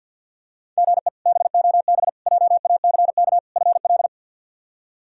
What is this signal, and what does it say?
Morse code, heard as a keyed tone, about 50 wpm, 700 Hz. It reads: GE BOB JACK FL
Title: K1USN SST QSO Practice - 50wpm